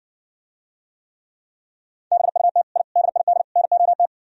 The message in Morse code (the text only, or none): BUT I DID NOT